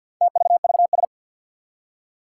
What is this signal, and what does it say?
Morse code, heard as a keyed tone, about 50 wpm, 700 Hz. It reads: TVVS
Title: Four Challenging Letter-Number Combinations 50wpm